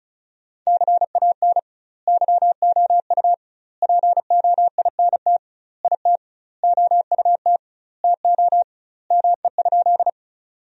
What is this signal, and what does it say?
Morse code, heard as a keyed tone, about 34 wpm, 700 Hz. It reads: CAN YOU POINT IT OUT TO ME?
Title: 2x Word-Spacing - Sentences from Top 200 Words - 35wpm